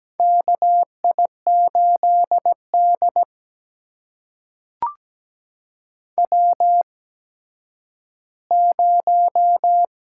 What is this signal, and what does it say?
Morse code, heard as a keyed tone, about 17 wpm, spaced as for 10 wpm, 700 Hz. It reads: KI8D W0
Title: USA Call Signs - 4 Characters - 8x Character Spacing - 17wpm